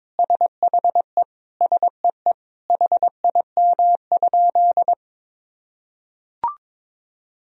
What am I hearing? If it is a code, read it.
Morse code, heard as a keyed tone, about 22 wpm, 700 Hz. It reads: SHE SEE HIM?
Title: Sentences from Top 100 Words and Abbreviations 22wpm